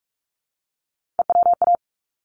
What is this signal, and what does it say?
Morse code, heard as a keyed tone, about 46 wpm, 700 Hz. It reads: EWA
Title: ARRL Sections 45wpm